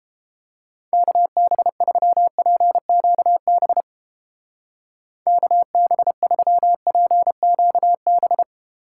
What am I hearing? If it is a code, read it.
Morse code, heard as a keyed tone, about 29 wpm, 700 Hz. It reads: KB3PQB KB3PQB